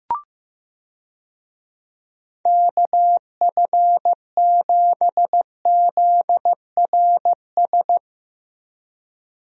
Morse code, heard as a keyed tone, about 15 wpm, 700 Hz. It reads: KF7ZRS
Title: USA Call Signs - 6 Characters - 15wpm